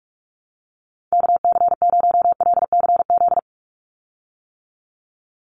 Morse code, heard as a keyed tone, about 48 wpm, 700 Hz. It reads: KC0LCZ
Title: USA Call Signs 45wpm